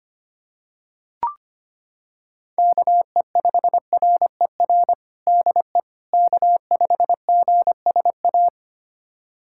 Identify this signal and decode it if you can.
Morse code, heard as a keyed tone, about 25 wpm, 700 Hz. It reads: KE5RER DE K5GSA
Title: USA Callsign DE Callsign 25wpm